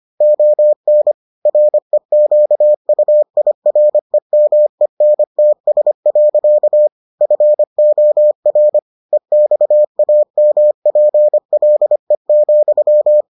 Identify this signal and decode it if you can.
Morse code, heard as a keyed tone, about 25 wpm, 600 Hz. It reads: ON REQUIREMENTS. FOR EXAMPLE,